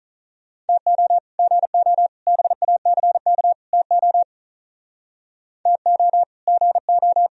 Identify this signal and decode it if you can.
Morse code, heard as a keyed tone, about 44 wpm, 700 Hz. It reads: TO GO BACK TO TO GO